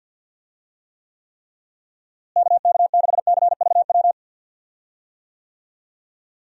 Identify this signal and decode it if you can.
Morse code, heard as a keyed tone, about 51 wpm, 700 Hz. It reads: KK6CVW
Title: USA Call Signs - 6 Characters - Repeated 3x - 50wpm